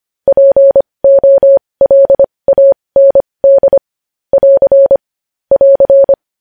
Morse code, heard as a keyed tone, about 25 wpm, spaced as for 20 wpm, 550 Hz. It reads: POLAND <AR> <AR>
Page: news-headlines-in-morse-code-at-20-wpm